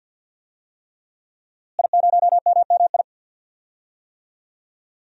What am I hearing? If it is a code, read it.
Morse code, heard as a keyed tone, about 50 wpm, 700 Hz. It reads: I0MMI
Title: DXCC-100 Call Signs - 5 Characters - Repeated 3x - 50wpm